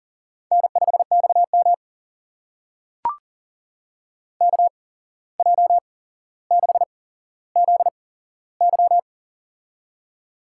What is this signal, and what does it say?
Morse code, heard as a keyed tone, about 40 wpm, spaced as for 16 wpm, 700 Hz. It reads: N5XM KJ6ZY
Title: USA Call Signs - 8x Character Spacing - 40wpm